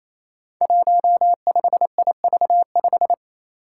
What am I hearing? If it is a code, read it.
Morse code, heard as a keyed tone, about 28 wpm, 700 Hz. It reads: 15IV5
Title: Five Challenging Letter-Number Combinations 28wpm